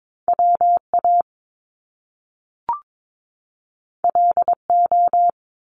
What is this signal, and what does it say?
Morse code, heard as a keyed tone, about 22 wpm, 700 Hz. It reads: WA LO
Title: Top Two Letter Combinations in Words 22wpm